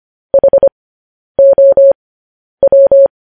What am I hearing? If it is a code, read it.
Morse code, heard as a keyed tone, about 25 wpm, spaced as for 10 wpm, 550 Hz. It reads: HOW
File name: podcast / News headlines-1x-25x10